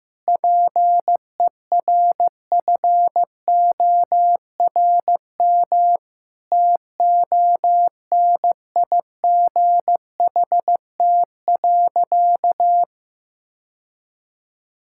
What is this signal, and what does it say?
Morse code, heard as a keyed tone, about 15 wpm, 700 Hz. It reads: PERFORM TONIGHT.
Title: Sentences from Top 1000 Words - Review Part 5 - 15wpm